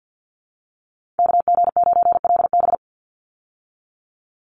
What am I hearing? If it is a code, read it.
Morse code, heard as a keyed tone, about 50 wpm, 700 Hz. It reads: KG9LB
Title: Call Signs Repeated 2x - 50wpm